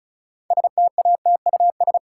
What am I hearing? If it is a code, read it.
Morse code, heard as a keyed tone, about 36 wpm, 700 Hz. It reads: STATUS
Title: Sentences from Top 1400 Words - New Word Drill - 35wpm